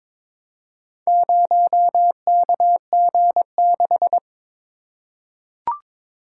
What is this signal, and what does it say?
Morse code, heard as a keyed tone, about 22 wpm, 700 Hz. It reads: 0KG6
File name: practice / 22wpm / Four Letter-Number Combinations - 22wpm